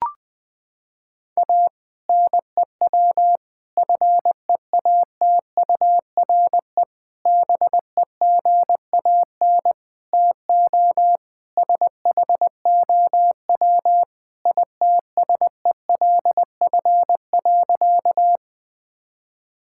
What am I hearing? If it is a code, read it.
Morse code, heard as a keyed tone, about 19 wpm, 700 Hz. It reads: A NEW FEATURE BEGAN TO SHOW ITSELF.
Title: Sentences from Top 1000 Words - Review Part 5 - 20wpm